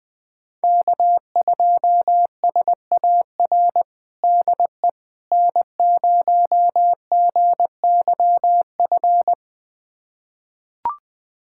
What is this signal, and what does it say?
Morse code, heard as a keyed tone, about 20 wpm, 700 Hz. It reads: K2SAR DE N0GYF
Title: USA Callsign DE Callsign - 5 Characters - 20wpm